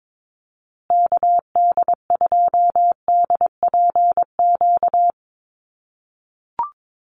Morse code, heard as a keyed tone, about 22 wpm, 700 Hz. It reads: KD2DPQ
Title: USA Call Signs Repeated 2x 22wpm